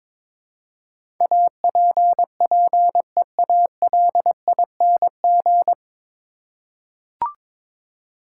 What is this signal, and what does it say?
Morse code, heard as a keyed tone, about 22 wpm, 700 Hz. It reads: APPEALING
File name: 9 Letter Words - First 3 Letters Given - 22wpm